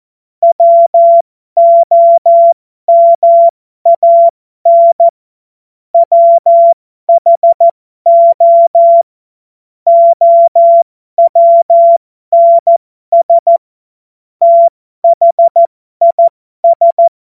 Morse code, heard as a keyed tone, about 14 wpm, spaced as for 12 wpm, 680 Hz. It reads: WOMAN WHO OWNS THIS